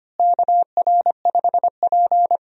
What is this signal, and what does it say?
Morse code, heard as a keyed tone, about 25 wpm, 700 Hz. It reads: KR5P
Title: USA Call Signs - 4 Characters - Repeated 3x - 25wpm